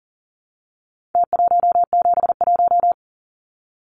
Morse code, heard as a keyed tone, about 40 wpm, 700 Hz. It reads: T171
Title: Numbers - Contest Style - 4-Digit 40wpm